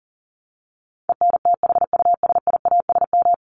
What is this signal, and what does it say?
Morse code, heard as a keyed tone, about 40 wpm, 700 Hz. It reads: ENTHUSIASM
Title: Top 1751-2000 Words 40wpm